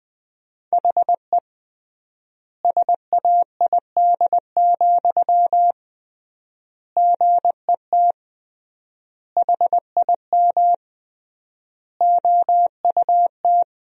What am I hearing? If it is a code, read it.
Morse code, heard as a keyed tone, about 20 wpm, 700 Hz. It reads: HE SAID, GET HIM OUT
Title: 3x Word-Spacing - Sentences from Top 400 Words - 20wpm